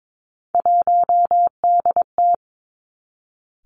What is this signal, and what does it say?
Morse code, heard as a keyed tone, about 22 wpm, 700 Hz. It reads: 1DT
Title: Three Challenging Letter-Number Combinations 22wpm